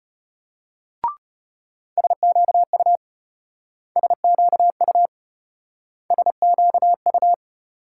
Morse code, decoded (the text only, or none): SQU SQU SQU